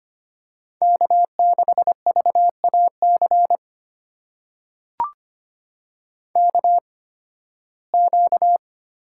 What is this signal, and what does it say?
Morse code, heard as a keyed tone, about 25 wpm, spaced as for 16 wpm, 700 Hz. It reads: K6VAC KQ
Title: USA Call Signs - 8x Character Spacing - 25wpm